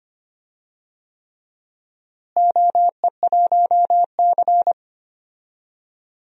Morse code, heard as a keyed tone, about 25 wpm, 700 Hz. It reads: OE1C
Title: DXCC-100 Call Signs - 4 Characters - Repeated 3x - 25wpm